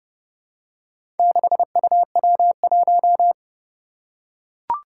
Morse code, heard as a keyed tone, about 30 wpm, 700 Hz. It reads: 6UW1
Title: Four Challenging Letter-Number Combinations 30wpm